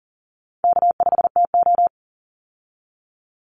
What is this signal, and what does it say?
Morse code, heard as a keyed tone, about 40 wpm, 700 Hz. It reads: K5TO
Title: Call Signs Repeated 3x - 40wpm